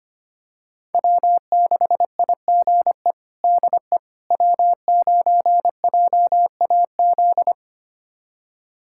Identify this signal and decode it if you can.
Morse code, heard as a keyed tone, about 25 wpm, 700 Hz. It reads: W6IGE DE W9JAZ
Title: USA Callsign DE Callsign - 5 Characters - 25wpm